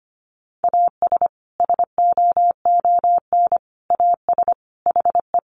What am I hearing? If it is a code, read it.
Morse code, heard as a keyed tone, about 25 wpm, 700 Hz. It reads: AS SOON AS HE
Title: Sentences from Top 1500 Words 25wpm